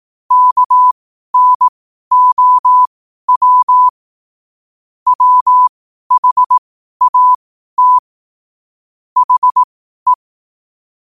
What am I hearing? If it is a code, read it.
Morse code, heard as a keyed tone, about 18 wpm, spaced as for 12 wpm, 1000 Hz. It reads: KNOW WHAT HE